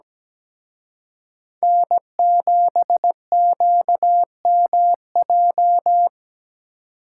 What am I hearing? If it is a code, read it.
Morse code, heard as a keyed tone, about 20 wpm, 700 Hz. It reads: N7QMJ
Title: Speed Racing - USA 1x3 Call Signs 17wpm